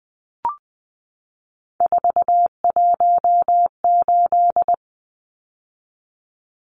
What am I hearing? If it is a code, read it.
Morse code, heard as a keyed tone, about 20 wpm, 700 Hz. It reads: 418
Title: Numbers - 3-Digit 20wpm